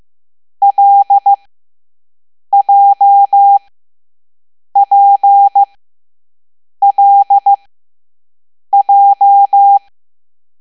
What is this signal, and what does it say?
Morse code, heard as a keyed tone, about 15 wpm, spaced as for 6 wpm, 800 Hz. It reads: LJPLJ
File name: Morse / Level7B